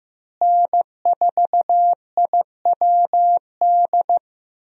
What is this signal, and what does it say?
Morse code, heard as a keyed tone, about 15 wpm, 700 Hz. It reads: N4IWD
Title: USA Call Signs - 5 Characters - Repeated 3x - 15wpm